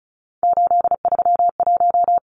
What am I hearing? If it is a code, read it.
Morse code, heard as a keyed tone, about 34 wpm, 700 Hz. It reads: 831
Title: Numbers - 3-Digit 35wpm